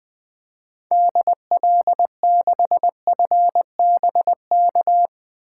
Morse code, heard as a keyed tone, about 20 wpm, 700 Hz. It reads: DL6FBK
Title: 2023 SOTA Activator Call Signs - 20wpm